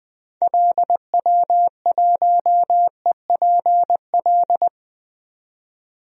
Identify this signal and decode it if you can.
Morse code, heard as a keyed tone, about 20 wpm, 700 Hz. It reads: LW1EPL
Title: DXCC-100 Call Signs - 6 Characters - 20wpm